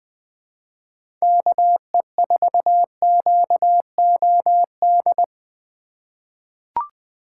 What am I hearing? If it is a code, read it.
Morse code, heard as a keyed tone, about 20 wpm, 700 Hz. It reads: KE4QOD
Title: USA Call Signs - 4x Character Spacing - 20wpm